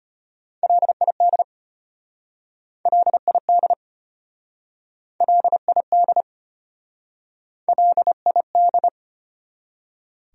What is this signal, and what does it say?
Morse code, heard as a keyed tone, about 33 wpm, 700 Hz. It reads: LID LID LID LID